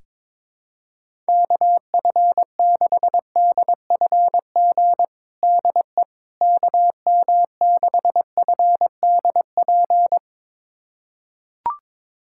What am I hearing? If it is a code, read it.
Morse code, heard as a keyed tone, about 22 wpm, 700 Hz. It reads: KF6DFG DE KM6FDP
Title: USA Callsign DE Callsign - 6 Characters - 22wpm